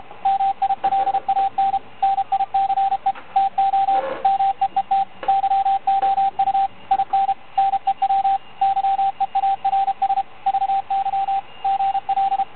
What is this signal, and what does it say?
Morse code, heard as a keyed tone, about 33 wpm, 780 Hz, with radio noise. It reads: MILAN NICE TO MEET YOU IN NEW YEARS VY GL